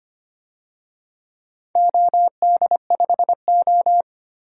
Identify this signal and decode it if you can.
Morse code, heard as a keyed tone, about 25 wpm, 700 Hz. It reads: OD5O
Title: DXCC-100 Call Signs - 4 Characters - 25wpm